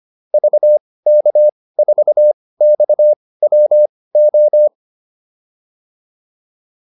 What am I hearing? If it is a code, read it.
Morse code, heard as a keyed tone, about 25 wpm, spaced as for 18 wpm, 600 Hz. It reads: VK4XWO
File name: QSO407 18